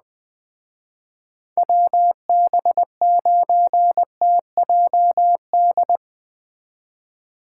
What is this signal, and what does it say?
Morse code, heard as a keyed tone, about 20 wpm, 700 Hz. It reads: WB9TJD